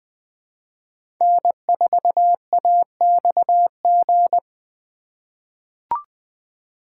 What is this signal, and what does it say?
Morse code, heard as a keyed tone, about 20 wpm, 700 Hz. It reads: N4AXG
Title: USA Call Signs - 5 Characters - 3x Character Spacing - 20wpm